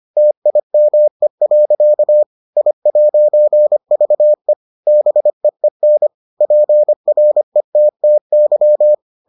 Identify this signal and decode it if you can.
Morse code, heard as a keyed tone, about 25 wpm, 600 Hz. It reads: TIME. I'VE BEEN PRETTY